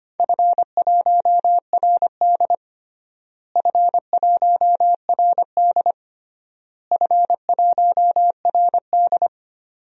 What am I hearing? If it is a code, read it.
Morse code, heard as a keyed tone, about 25 wpm, 700 Hz. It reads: F1RD F1RD F1RD
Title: DXCC-100 Call Signs - 4 Characters - Repeated 3x - 25wpm